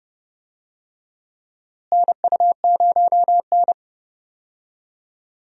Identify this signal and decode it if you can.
Morse code, heard as a keyed tone, about 29 wpm, 700 Hz. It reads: NU0N
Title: USA Call Signs - 4 Characters - Repeated 2x - 30wpm